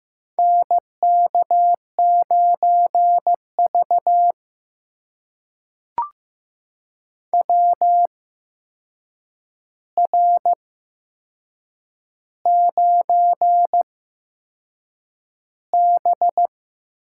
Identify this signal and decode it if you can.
Morse code, heard as a keyed tone, about 15 wpm, spaced as for 6 wpm, 700 Hz. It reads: NK9V WR9B
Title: USA Call Signs - 4 Characters - 8x Character Spacing - 15wpm